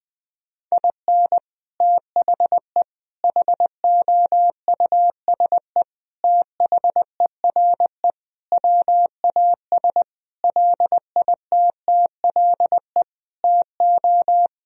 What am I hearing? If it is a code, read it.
Morse code, heard as a keyed tone, about 20 wpm, 700 Hz. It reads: IN THE HOUSE THERE WAS LITTLE TO
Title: Sentences from Top 1000 Words - Review Part 2 - 20wpm